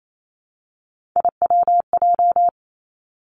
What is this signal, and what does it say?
Morse code, heard as a keyed tone, about 28 wpm, 700 Hz. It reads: IWJ
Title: Three Challenging Letter-Number Combinations 28wpm